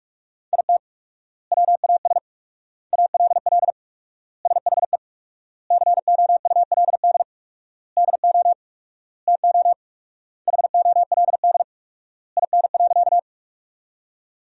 Morse code, heard as a keyed tone, about 48 wpm, 700 Hz. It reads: IT WAS ALL SHE COULD DO TO HOLD IN.
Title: 4x Word-Spacing - Sentences from Top 400 Words - 45wpm